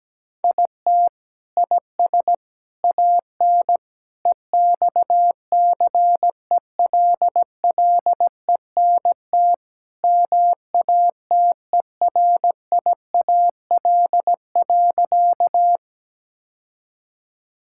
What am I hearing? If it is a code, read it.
Morse code, heard as a keyed tone, about 17 wpm, 700 Hz. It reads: IT IS AN EXCELLENT MATERIAL.
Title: Sentences from Top 1000 Words - Review Part 5 - 17wpm